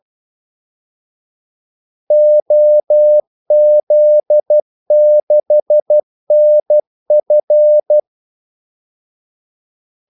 Morse code, heard as a keyed tone, about 12 wpm, 600 Hz. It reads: OZ6NF
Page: 2 ekstra ordmellemrum mellem kaldesignalerne